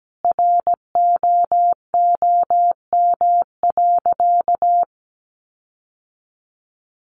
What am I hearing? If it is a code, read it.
Morse code, heard as a keyed tone, about 17 wpm, 700 Hz. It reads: ROOM.
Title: Sentences from Top 1300 Words 17wpm